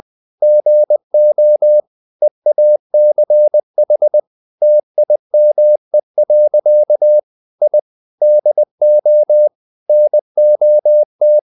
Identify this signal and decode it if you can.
Morse code, heard as a keyed tone, about 20 wpm, 600 Hz. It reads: GO EACH TIME. I DO NOT
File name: the old man and the sea part 2 20wpm